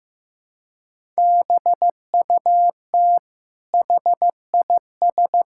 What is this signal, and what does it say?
Morse code, heard as a keyed tone, about 15 wpm, 700 Hz. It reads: BUT HIS
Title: Sentences from Top 1000 Words - Review Part 6 - 15wpm